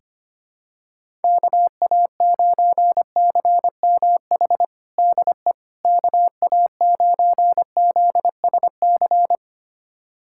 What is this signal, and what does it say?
Morse code, heard as a keyed tone, about 25 wpm, 700 Hz. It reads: KA9CMH DE KA9ZSC
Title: USA Callsign DE Callsign 25wpm